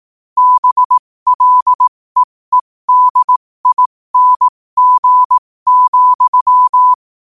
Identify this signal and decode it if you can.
Morse code, heard as a keyed tone, about 18 wpm, spaced as for 15 wpm, 1000 Hz. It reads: BLEEDING,
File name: qotdfiles / 15wpm